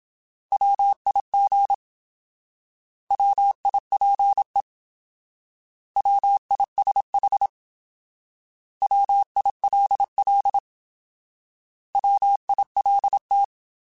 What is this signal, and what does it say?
Morse code, heard as a keyed tone, about 26 wpm, 800 Hz. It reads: WIG WIPE WISH WILL WILT
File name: Alphabetical / Alphabetical mp3s / Ws @25